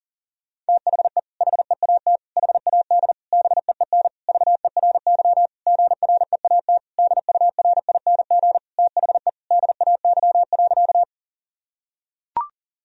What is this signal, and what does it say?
Morse code, heard as a keyed tone, about 40 wpm, 700 Hz. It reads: THE HEAT HAD BEEN VERY GREAT DURING THE DAY.
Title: Sentences from Top 1000 Words - Review Part 3 - 40wpm